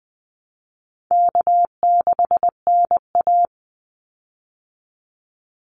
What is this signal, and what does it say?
Morse code, heard as a keyed tone, about 20 wpm, 700 Hz. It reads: K6NA
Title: Easy Call Signs Repeated 2x - 20wpm